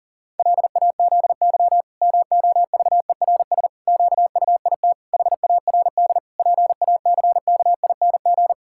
Morse code, heard as a keyed tone, about 37 wpm, 700 Hz. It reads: LAZY MOVERS QUIT HARD PACKING
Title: Pangram Challenge v2 - 40wpm